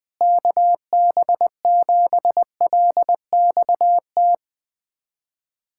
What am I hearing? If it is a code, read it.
Morse code, heard as a keyed tone, about 21 wpm, 700 Hz. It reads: KB7LXT
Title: USA Callsign DE Callsign - 6 Characters - 20wpm